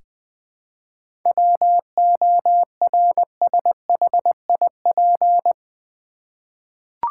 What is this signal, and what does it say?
Morse code, heard as a keyed tone, about 20 wpm, 700 Hz. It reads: WORSHIP
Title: 7 Letter Words - First 3 Letters Given - 20wpm